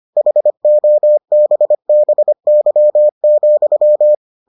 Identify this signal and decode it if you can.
Morse code, heard as a keyed tone, about 25 wpm, 600 Hz. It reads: HOBBY,